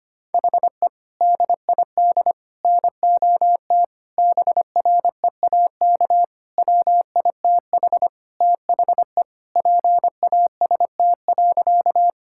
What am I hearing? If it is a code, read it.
Morse code, heard as a keyed tone, about 25 wpm, 700 Hz. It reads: HE DID NOT BREAK WITH THE PAST.
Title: Sentences from Top 1000 Words - Review Part 6 - 25wpm